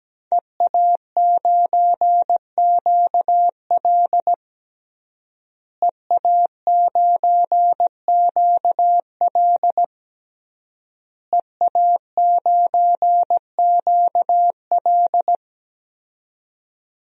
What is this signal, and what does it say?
Morse code, heard as a keyed tone, about 17 wpm, 700 Hz. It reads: EA9QL EA9QL EA9QL
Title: DXCC-100 Call Signs - 5 Characters - Repeated 3x - 17wpm